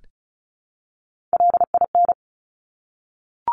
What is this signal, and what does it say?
Morse code, heard as a keyed tone, about 36 wpm, 700 Hz. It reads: LIN
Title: Top Three Letter Combinations in Words 35wpm